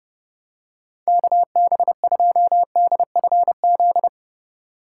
Morse code, heard as a keyed tone, about 30 wpm, 700 Hz. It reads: KB2DFZ
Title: USA Call Signs - 2x Character Spacing - 30wpm